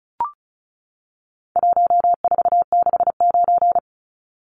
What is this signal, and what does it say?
Morse code, heard as a keyed tone, about 34 wpm, 700 Hz. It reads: 1469
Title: Numbers - 4-Digit 35wpm